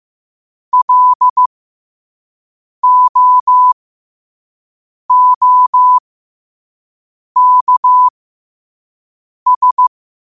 Morse code, heard as a keyed tone, about 15 wpm, spaced as for 5 wpm, 1000 Hz. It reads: LOOKS
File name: qotdfiles / 5wpm